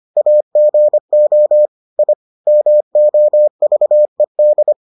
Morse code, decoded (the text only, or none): AGO I MOVED